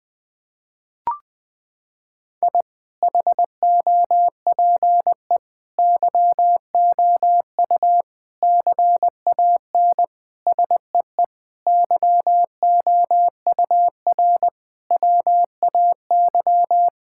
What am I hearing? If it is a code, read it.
Morse code, heard as a keyed tone, about 19 wpm, 700 Hz. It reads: I HOPE YOU CAN SEE YOUR WAY
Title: Sentences with 9-Word Phrases - 20wpm